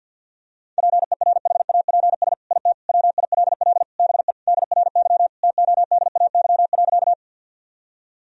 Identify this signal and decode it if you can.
Morse code, heard as a keyed tone, about 50 wpm, 700 Hz. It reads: PERHAPS IT WILL BE DRY TODAY.
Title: Sentences from Top 1000 Words - Review Part 3 - 50wpm